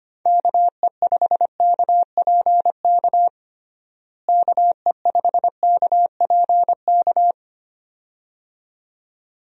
Morse code, heard as a keyed tone, about 25 wpm, 700 Hz. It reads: KE5KPK KE5KPK
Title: USA Call Signs - 6 Characters - Repeated 2x - 25wpm